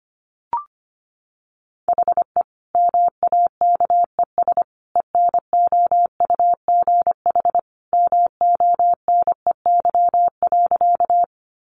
Morse code, heard as a keyed tone, about 25 wpm, 700 Hz. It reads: HE MAKES ENOUGH MONEY.
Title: Additional Easy Phrases 25wpm